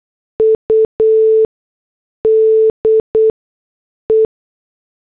Morse code, heard as a keyed tone, about 8 wpm, spaced as for 6 wpm, 440 Hz. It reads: UDE
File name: morse / headlines